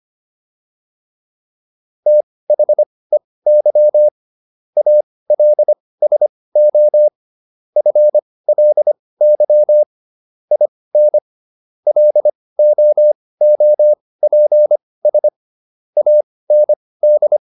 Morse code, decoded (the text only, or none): THEY ALSO FLY IN LOOPS AND